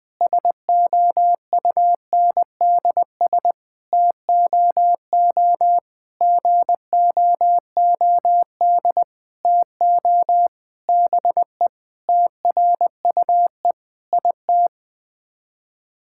Morse code, heard as a keyed tone, about 20 wpm, 700 Hz. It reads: SOUNDS TOO GOOD TO BE TRUE IT
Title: Sets of 9 Words - 20wpm